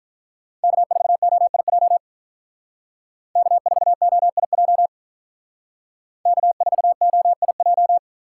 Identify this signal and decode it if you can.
Morse code, heard as a keyed tone, about 43 wpm, 700 Hz. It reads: K4OIJ K4OIJ K4OIJ
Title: Speed Racing - USA Call Signs 35wpm